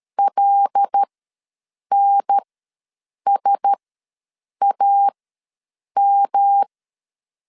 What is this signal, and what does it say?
Morse code, heard as a keyed tone, about 13 wpm, spaced as for 7 wpm, 800 Hz. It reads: LNSAM
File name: LEZIONI / 5 LBVC / 5 LBV ws 12wpm 3015